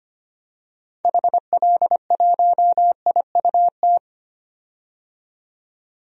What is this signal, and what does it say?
Morse code, heard as a keyed tone, about 25 wpm, 700 Hz. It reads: HL1IUT
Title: DXCC-100 Call Signs - 6 Characters - 25wpm